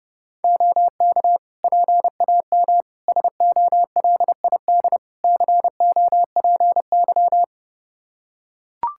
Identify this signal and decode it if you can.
Morse code, heard as a keyed tone, about 30 wpm, 700 Hz. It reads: OK PAM SOLID COPY
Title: K1USN SST QSO Practice 30wpm